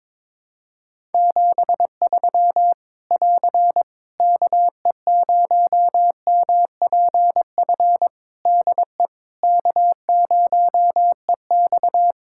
Morse code, heard as a keyed tone, about 22 wpm, 700 Hz. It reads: 73 <AR> KE0MPF DE K0EX
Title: Practice QSO - Exchange 4 - Ending - 22wpm